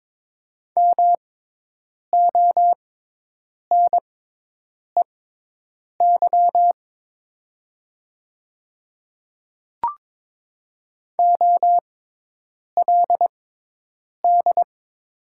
Morse code, heard as a keyed tone, about 22 wpm, spaced as for 8 wpm, 700 Hz. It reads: MONEY OLD
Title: 6x Character Spacing - Top 500 Words - 22wpm